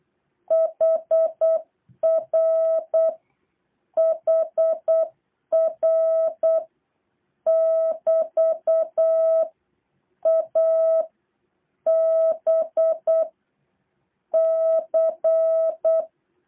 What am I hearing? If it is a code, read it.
Morse code, heard as a keyed tone, about 8 wpm, 650 Hz. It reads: HR HR <BT> A B C
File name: jh8cbh / 2023cw